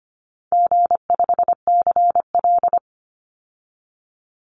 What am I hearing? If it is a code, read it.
Morse code, heard as a keyed tone, about 25 wpm, 700 Hz. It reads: G5CL
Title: Easy Call Signs Repeated 3x - 25wpm